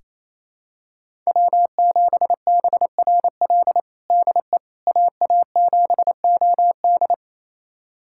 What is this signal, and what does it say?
Morse code, heard as a keyed tone, about 28 wpm, 700 Hz. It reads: W7BRL DE AA7OD
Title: USA Callsign DE Callsign - 5 Characters - 28wpm